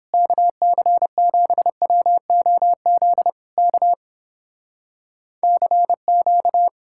Morse code, heard as a keyed tone, about 30 wpm, 700 Hz. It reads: KC7WOZ K CQ